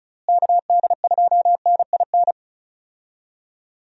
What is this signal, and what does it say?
Morse code, heard as a keyed tone, about 34 wpm, 700 Hz. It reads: KD2NIN
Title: USA Call Signs - 6 Characters - Repeated 3x - 35wpm